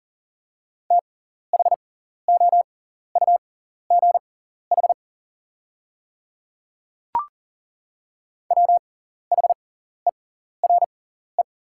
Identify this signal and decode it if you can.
Morse code, heard as a keyed tone, about 40 wpm, spaced as for 14 wpm, 700 Hz. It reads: THOUGH WHERE
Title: 6x Character Spacing - Top 500 Words - 40wpm